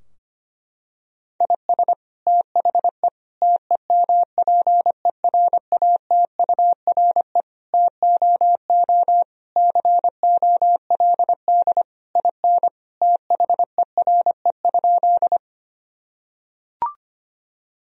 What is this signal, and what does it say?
Morse code, heard as a keyed tone, about 25 wpm, 700 Hz. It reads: IS THE TEMPERATURE TOO COLD IN THERE?
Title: Sentences from Top 1000 Words - Review Part 4 - 25wpm